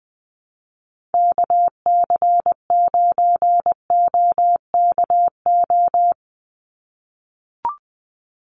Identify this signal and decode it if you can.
Morse code, heard as a keyed tone, about 20 wpm, 700 Hz. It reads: KC9OKO
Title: USA Call Signs 20wpm